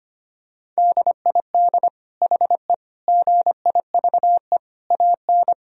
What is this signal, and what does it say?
Morse code, heard as a keyed tone, about 25 wpm, 700 Hz. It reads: DID HE GIVE AN
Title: Sentences from Top 1000 Words - Review Part 3 - 25wpm